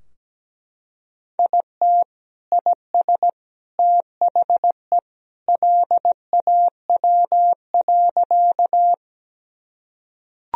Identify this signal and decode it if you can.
Morse code, heard as a keyed tone, about 17 wpm, 700 Hz. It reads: IT IS THE LAW.
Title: Sentences from Top 1000 Words - Review Part 3 - 17wpm